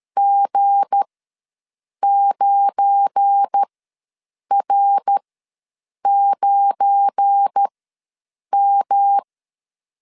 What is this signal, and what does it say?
Morse code, heard as a keyed tone, about 13 wpm, spaced as for 7 wpm, 800 Hz. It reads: G9R9M
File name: LEZIONI / 7 GF9Z / 7 GF9 ws 12wpm 3344